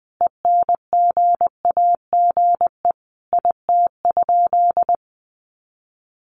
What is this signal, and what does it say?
Morse code, heard as a keyed tone, about 20 wpm, 700 Hz. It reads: ENGAGE IT?
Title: Sentences from Top 1500 Words 20wpm